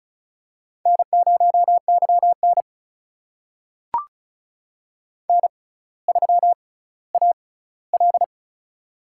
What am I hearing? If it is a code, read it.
Morse code, heard as a keyed tone, about 36 wpm, spaced as for 18 wpm, 700 Hz. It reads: N0YN N3AL
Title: USA Call Signs - 4 Characters - 6x Character Spacing - 35wpm